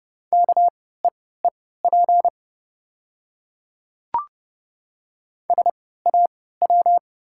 Morse code, heard as a keyed tone, about 29 wpm, spaced as for 17 wpm, 700 Hz. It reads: KEEP SAW
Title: 3x Character Spacing - Top 500 Words - 30wpm